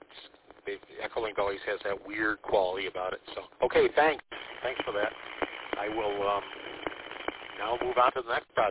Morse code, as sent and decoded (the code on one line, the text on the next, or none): none